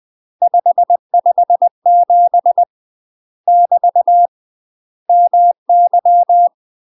Morse code, heard as a keyed tone, about 20 wpm, 700 Hz. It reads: 557 <BT> MY
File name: UK MASTER QSOs TRAINING / 20WPM / UK MASTER QSOs 0011